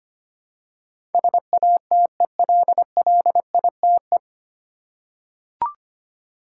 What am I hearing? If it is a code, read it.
Morse code, heard as a keyed tone, about 25 wpm, 700 Hz. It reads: SATELLITE
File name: practice / 25wpm / 9 Letter Words - First 3 Letters Given - 25wpm